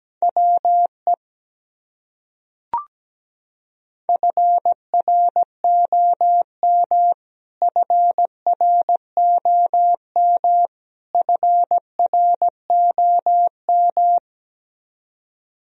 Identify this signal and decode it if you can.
Morse code, heard as a keyed tone, about 17 wpm, 700 Hz. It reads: WE FROM FROM FROM
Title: Top 100 Words Repeated 17wpm